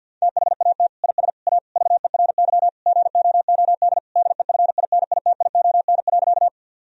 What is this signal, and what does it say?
Morse code, heard as a keyed tone, about 50 wpm, 700 Hz. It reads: THAT IS A VERY GOOD DEFINITION.
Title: Sentences from Top 1000 Words - Review Part 4 - 50wpm